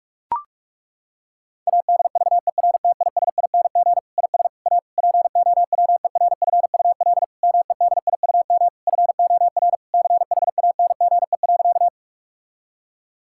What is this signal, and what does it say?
Morse code, heard as a keyed tone, about 41 wpm, 700 Hz. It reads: ADVERTISING IS A POWERFUL MEDIUM FOR CHANGE.
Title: Sentences from Top 1000 Words - Review Part 6 - 45wpm